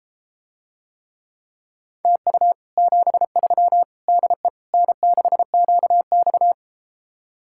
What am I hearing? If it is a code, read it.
Morse code, heard as a keyed tone, about 33 wpm, 700 Hz. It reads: TU 73 DE N6QX